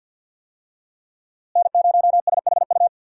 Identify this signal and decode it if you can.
Morse code, heard as a keyed tone, about 53 wpm, 700 Hz. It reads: N0SHU
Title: USA Call Signs - 5 Characters - Repeated 2x - 50wpm